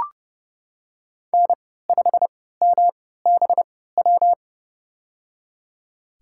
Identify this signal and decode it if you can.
Morse code, heard as a keyed tone, about 30 wpm, spaced as for 17 wpm, 700 Hz. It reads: N5MBW
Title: USA Call Signs - 5 Characters - 3x Character Spacing - 30wpm